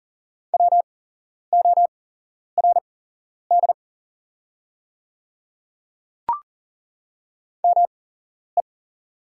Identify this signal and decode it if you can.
Morse code, heard as a keyed tone, about 37 wpm, spaced as for 11 wpm, 700 Hz. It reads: WORD ME
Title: 8x Character Spacing - Top 200 Words - 40wpm